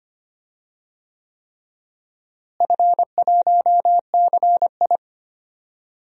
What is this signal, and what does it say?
Morse code, heard as a keyed tone, about 25 wpm, 700 Hz. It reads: F1CI
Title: DXCC-100 Call Signs - 4 Characters - Repeated 3x - 25wpm